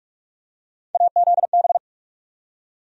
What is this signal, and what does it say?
Morse code, heard as a keyed tone, about 46 wpm, 700 Hz. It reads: AZB
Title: USA Call Sign Postfixes - 3 Letters 45wpm